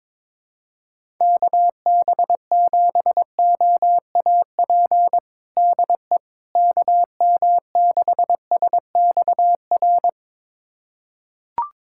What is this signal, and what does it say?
Morse code, heard as a keyed tone, about 22 wpm, 700 Hz. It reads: KB7OAP DE KM6SXR
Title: USA Callsign DE Callsign - 6 Characters - 22wpm